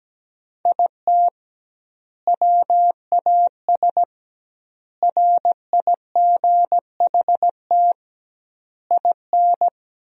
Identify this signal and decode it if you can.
Morse code, heard as a keyed tone, about 17 wpm, 700 Hz. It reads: IT WAS RIGHT IN
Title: 2x Word-Spacing - Sentences from Top 500 Words - 17wpm